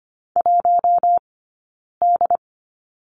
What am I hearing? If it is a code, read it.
Morse code, heard as a keyed tone, about 25 wpm, 700 Hz. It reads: 1 D
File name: Lesson 15d - Number 5 - Two Characters 25wpm